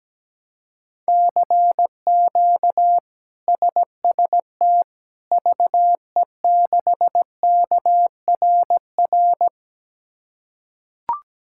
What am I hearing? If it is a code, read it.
Morse code, heard as a keyed tone, about 17 wpm, 700 Hz. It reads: CQ SST VE6KRR
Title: K1USN SST QSO Practice - 17wpm